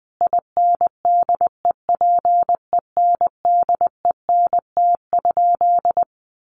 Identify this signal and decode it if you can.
Morse code, heard as a keyed tone, about 20 wpm, 700 Hz. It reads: INDEPENDENT?
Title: Sentences from Top 1400 Words 20wpm